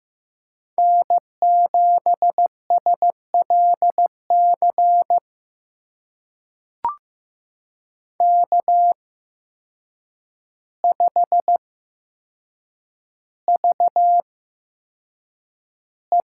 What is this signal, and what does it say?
Morse code, heard as a keyed tone, about 15 wpm, spaced as for 7 wpm, 700 Hz. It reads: N7SLC K5VE
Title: USA Call Signs - 5 Characters - 8x Character Spacing - 15wpm